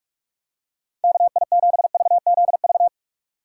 Morse code, heard as a keyed tone, about 45 wpm, 690 Hz. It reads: KI7VZV
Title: USA Call Signs - 6 Characters - 2x Character Spacing - 45wpm